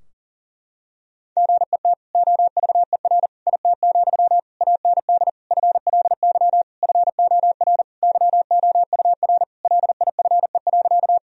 Morse code, heard as a keyed tone, about 40 wpm, 700 Hz. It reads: GET OVER IT, AND FLY FOR YOUR LIFE.
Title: Sentences from Top 1000 Words - Review Part 2 - 40wpm